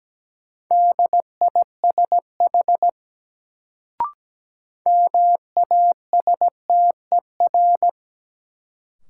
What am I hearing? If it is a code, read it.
Morse code, heard as a keyed tone, about 17 wpm, 700 Hz. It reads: DISH MASTER
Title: Sentences from Top 1300 Words - New Word Drill Alt - 17wpm